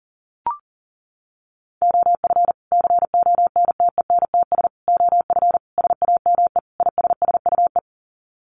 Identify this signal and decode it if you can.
Morse code, heard as a keyed tone, about 40 wpm, 700 Hz. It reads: OF CONTENTS OF SAME ISSUE
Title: Sets of 5 Words Encore2 40wpm